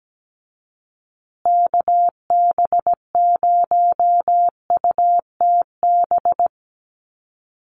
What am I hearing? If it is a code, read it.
Morse code, heard as a keyed tone, about 17 wpm, 700 Hz. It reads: KB0UTB
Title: USA Call Signs Repeated 3x 17wpm